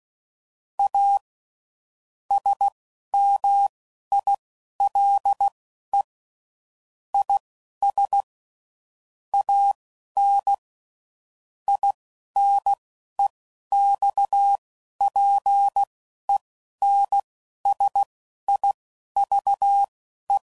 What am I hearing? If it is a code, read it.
Morse code, heard as a keyed tone, about 16 wpm, spaced as for 12 wpm, 800 Hz. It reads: A SMILE IS AN INEXPENSIVE
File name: Sayings / Quotes / A Smile